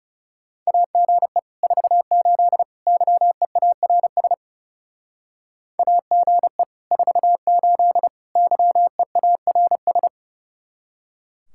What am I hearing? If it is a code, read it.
Morse code, heard as a keyed tone, about 36 wpm, 700 Hz. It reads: AGE 48 YEARS AGE 48 YEARS
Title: Speed-Racing Common QSO Phrases - 30wpm